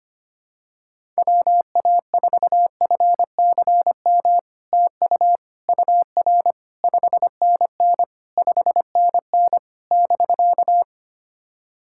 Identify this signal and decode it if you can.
Morse code, heard as a keyed tone, about 25 wpm, 700 Hz. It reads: WA4FCM TU UR 5NN 5NN <BK>
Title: SOTA QSO Practice 25wpm